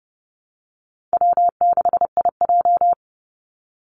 Morse code, heard as a keyed tone, about 30 wpm, 700 Hz. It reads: W6IJ
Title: Call Signs - 30wpm